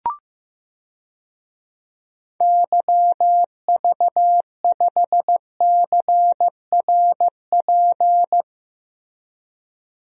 Morse code, heard as a keyed tone, about 15 wpm, 700 Hz. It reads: YV5CRP
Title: DXCC-100 Call Signs - 6 Characters - Repeated 3x - 15wpm